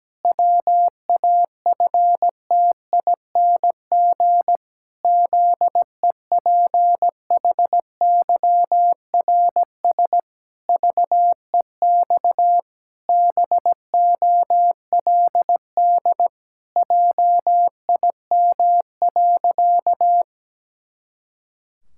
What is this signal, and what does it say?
Morse code, heard as a keyed tone, about 17 wpm, 700 Hz. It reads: WAFTING ZEPHYRS VEX BOLD JIM.
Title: Pangram Challenge v2 - 17wpm